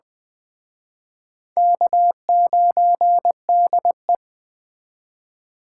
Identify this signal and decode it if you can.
Morse code, heard as a keyed tone, about 20 wpm, 700 Hz. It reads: K9DE
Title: Speed Racing - USA 1x2 Call Signs 17wpm